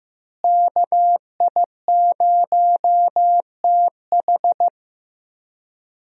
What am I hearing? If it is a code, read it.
Morse code, heard as a keyed tone, about 15 wpm, 700 Hz. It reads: KI0TH
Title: USA Call Signs - 5 Characters - Repeated 3x - 15wpm